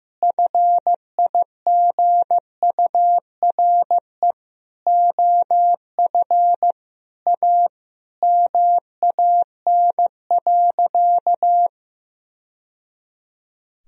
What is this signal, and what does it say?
Morse code, heard as a keyed tone, about 14 wpm, 700 Hz. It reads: FIGURE OF A MAN.
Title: Sentences from Top 1000 Words - Review Part 2 - 15wpm